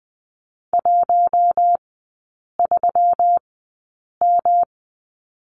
Morse code, heard as a keyed tone, about 20 wpm, spaced as for 8 wpm, 700 Hz. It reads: 13M
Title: 121 Lesson 20e - Number 3 - Three Characters 20wpm